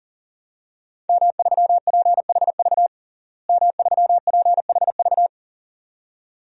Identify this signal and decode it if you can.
Morse code, heard as a keyed tone, about 40 wpm, 700 Hz. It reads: M3PHV M3PHV
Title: DXCC-100 Call Signs - 5 Characters - Repeated 2x - 40wpm